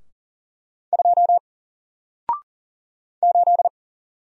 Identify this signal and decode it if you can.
Morse code, heard as a keyed tone, about 40 wpm, 700 Hz. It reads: 2 8
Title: Single Letter-Number Alt 40wpm